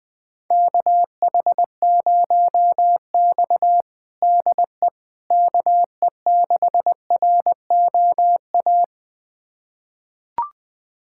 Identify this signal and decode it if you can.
Morse code, heard as a keyed tone, about 20 wpm, 700 Hz. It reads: KH0X DE KE6ROA
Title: USA Callsign DE Callsign 20wpm